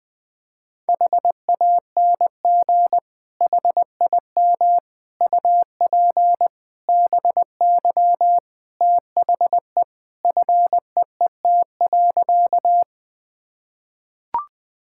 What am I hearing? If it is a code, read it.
Morse code, heard as a keyed tone, about 20 wpm, 700 Hz. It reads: HANG HIM UP BY THE FEET.
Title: Sentences from Top 1600 Words 20wpm